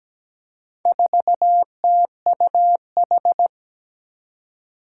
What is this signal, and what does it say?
Morse code, heard as a keyed tone, about 17 wpm, 700 Hz. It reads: 4TUH
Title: Four Challenging Letter-Number Combinations 17wpm